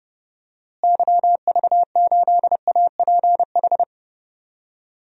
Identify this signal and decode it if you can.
Morse code, heard as a keyed tone, about 32 wpm, 700 Hz. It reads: YV8APH
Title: DXCC-100 Call Signs - 6 Characters - Repeated 3x - 30wpm